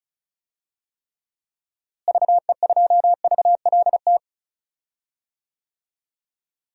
Morse code, heard as a keyed tone, about 34 wpm, 700 Hz. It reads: VE2VLT
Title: DXCC-100 Call Signs - 6 Characters - Repeated 2x - 35wpm